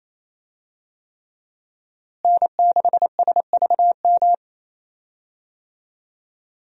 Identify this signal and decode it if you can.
Morse code, heard as a keyed tone, about 28 wpm, 700 Hz. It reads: N6SVM
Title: USA Call Signs - 5 Characters - Repeated 3x - 28wpm